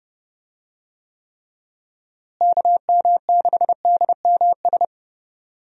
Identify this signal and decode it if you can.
Morse code, heard as a keyed tone, about 30 wpm, 700 Hz. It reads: KM6DMS
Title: USA Call Signs - 6 Characters - Repeated 3x - 30wpm